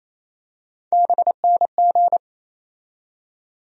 Morse code, heard as a keyed tone, about 27 wpm, 700 Hz. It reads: BNG
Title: USA Call Sign Postfixes - 3 Letters 28wpm